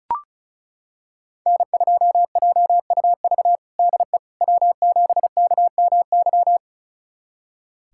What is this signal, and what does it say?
Morse code, heard as a keyed tone, about 34 wpm, 700 Hz. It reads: N2JUV DE W7KMY
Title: USA Callsign DE Callsign - 5 Characters - 35wpm